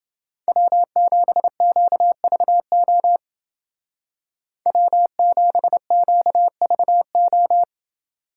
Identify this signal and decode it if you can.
Morse code, heard as a keyed tone, about 28 wpm, 700 Hz. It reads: W7QVO W7QVO